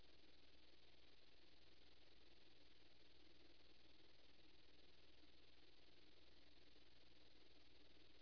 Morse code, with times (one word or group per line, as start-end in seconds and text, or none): none